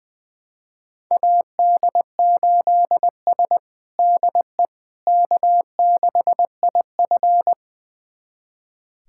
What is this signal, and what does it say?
Morse code, heard as a keyed tone, about 20 wpm, 700 Hz. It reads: AD8S DE K6IF
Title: USA Callsign DE Callsign - 4 Characters - 20wpm